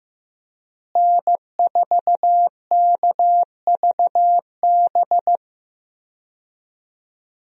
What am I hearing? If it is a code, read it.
Morse code, heard as a keyed tone, about 15 wpm, 700 Hz. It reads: N4KVB
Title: USA Call Signs - 5 Characters - Repeated 3x - 15wpm